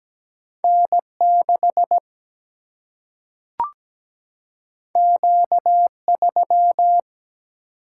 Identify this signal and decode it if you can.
Morse code, heard as a keyed tone, about 17 wpm, 700 Hz. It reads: N6 Q3
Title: Single-Letter Single-Number 17wpm